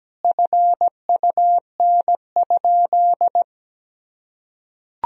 Morse code, heard as a keyed tone, about 17 wpm, 700 Hz. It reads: FUN?
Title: Sentences from Top 1000 Words - Review Part 3 - 17wpm